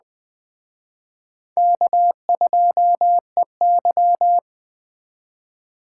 Morse code, heard as a keyed tone, about 21 wpm, 700 Hz. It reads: K2EY
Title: Speed Racing - USA 1x2 Call Signs 17wpm